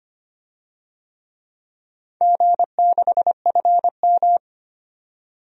Morse code, heard as a keyed tone, about 25 wpm, 700 Hz. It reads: G6FM
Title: DXCC-100 Call Signs - 4 Characters - 25wpm